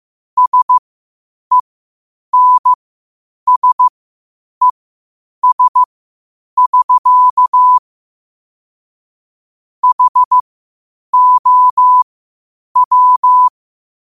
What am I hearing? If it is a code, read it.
Morse code, heard as a keyed tone, about 15 wpm, spaced as for 8 wpm, 1000 Hz. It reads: SENSES<SK> HOW
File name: qotdfiles / 8wpm